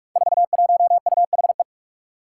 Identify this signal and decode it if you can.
Morse code, heard as a keyed tone, about 44 wpm, 700 Hz. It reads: 41UHE
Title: Five Challenging Letter-Number Combinations 45wpm